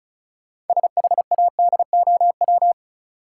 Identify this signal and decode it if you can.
Morse code, heard as a keyed tone, about 36 wpm, 700 Hz. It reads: SHADOW
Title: Sets of 5 Words Encore - New Word Drill - 35wpm